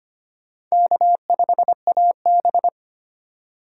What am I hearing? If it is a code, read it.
Morse code, heard as a keyed tone, about 32 wpm, 700 Hz. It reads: K5AB